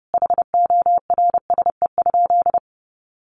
Morse code, heard as a keyed tone, about 30 wpm, 700 Hz. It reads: HORSE?
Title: Sentences from Top 1400 Words 30wpm